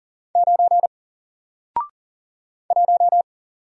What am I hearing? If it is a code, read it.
Morse code, heard as a keyed tone, about 38 wpm, 700 Hz. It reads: 9 1
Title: Single Letter-Number Alt 40wpm